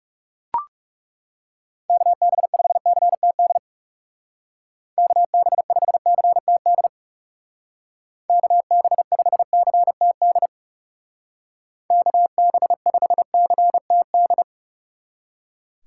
Morse code, heard as a keyed tone, about 35 wpm, 700 Hz. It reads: KB5CTD KB5CTD KB5CTD KB5CTD
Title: Speed Racing - USA 2x3 Call Signs 30wpm